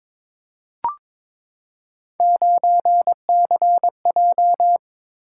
Morse code, heard as a keyed tone, about 22 wpm, 700 Hz. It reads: 9CJ
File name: practice / 22wpm / Three Letter-Number Combinations - 22wpm